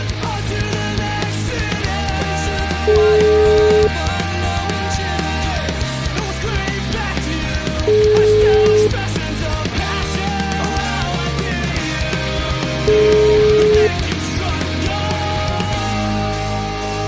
Ringback tone (Europe, 425 Hz), with noise.